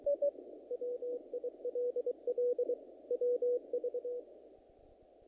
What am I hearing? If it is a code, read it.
Morse code, heard as a keyed tone, about 23 wpm, 490 Hz. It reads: WILL WV
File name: contests / naqp-cw-k5tr-op-k5pi- -2000